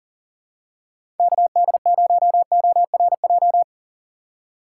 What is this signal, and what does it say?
Morse code, heard as a keyed tone, about 39 wpm, 700 Hz. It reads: KD0ORJ
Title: USA Call Signs - 6 Characters - 2x Character Spacing - 40wpm